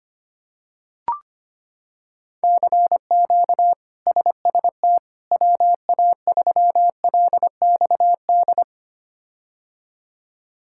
Morse code, heard as a keyed tone, about 25 wpm, 700 Hz. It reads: CQ SST WA3LXD
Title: K1USN SST QSO Practice - 25wpm